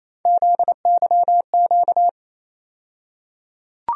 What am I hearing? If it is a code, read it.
Morse code, heard as a keyed tone, about 28 wpm, 700 Hz. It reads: ZYQ
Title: USA Call Sign Postfixes - 3 Letters 28wpm